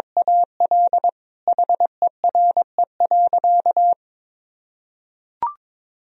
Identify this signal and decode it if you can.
Morse code, heard as a keyed tone, about 22 wpm, 700 Hz. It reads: AL HERE.
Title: Sentences from Top 1000 Words - Review Part 4 - 22wpm